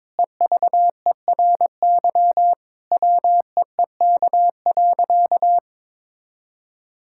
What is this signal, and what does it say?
Morse code, heard as a keyed tone, about 22 wpm, 700 Hz. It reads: EVERY WEEK.
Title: Sentences from Top 1000 Words - Review Part 5 - 22wpm